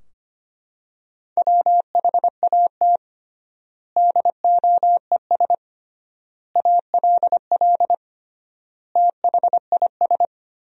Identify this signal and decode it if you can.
Morse code, heard as a keyed tone, about 25 wpm, 700 Hz. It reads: WHAT DOES ALL THIS
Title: 3x Word-Spacing - Sentences from Top 200 Words - 25wpm